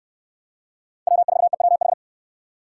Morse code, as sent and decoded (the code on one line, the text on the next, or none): .-- ....- . .-- ....
W4EWH